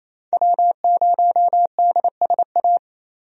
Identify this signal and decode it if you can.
Morse code, heard as a keyed tone, about 28 wpm, 700 Hz. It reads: W0DSA
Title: USA Call Signs - 5 Characters - 28wpm